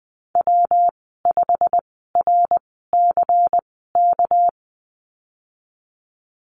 Morse code, heard as a keyed tone, about 20 wpm, spaced as for 15 wpm, 700 Hz. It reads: W5RCK
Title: 231 Lesson 37g - Letter x - Call Signs 20wpm